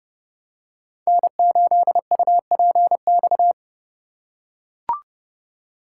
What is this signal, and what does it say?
Morse code, heard as a keyed tone, about 31 wpm, 700 Hz. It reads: N8UPX
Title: USA Call Signs - 5 Characters - 4x Character Spacing - 30wpm